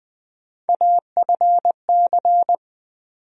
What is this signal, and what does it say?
Morse code, heard as a keyed tone, about 20 wpm, 700 Hz. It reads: AFC
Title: USA Call Sign Postfixes - 3 Letters 20wpm